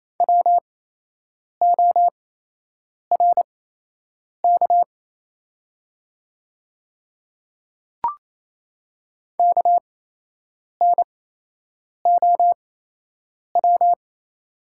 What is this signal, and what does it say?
Morse code, heard as a keyed tone, about 28 wpm, spaced as for 8 wpm, 700 Hz. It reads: WORK KNOW
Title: 8x Character Spacing - Top 300 Words - 28wpm